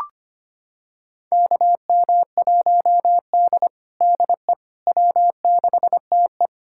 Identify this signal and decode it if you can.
Morse code, heard as a keyed tone, about 25 wpm, 700 Hz. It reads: KM1D DE W6TE
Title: USA Callsign DE Callsign - 4 Characters - 25wpm